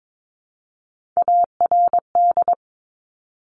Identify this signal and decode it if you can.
Morse code, heard as a keyed tone, about 22 wpm, 700 Hz. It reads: ARD
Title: Top Three Letter Combinations in Words 22wpm